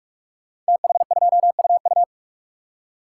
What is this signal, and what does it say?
Morse code, heard as a keyed tone, about 44 wpm, 700 Hz. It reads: TH2UU
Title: Code Groups - 5 Letter:Numbers - 45wpm